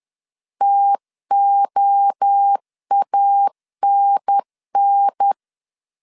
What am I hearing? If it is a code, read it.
Morse code, heard as a keyed tone, about 10 wpm, 800 Hz. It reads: TOANN
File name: CW / DET / 1 3 1500C S C 10wpm 3335